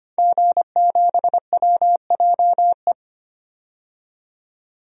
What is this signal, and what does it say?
Morse code, heard as a keyed tone, about 25 wpm, 700 Hz. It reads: G7WJE
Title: DXCC-100 Call Signs - 5 Characters - 25wpm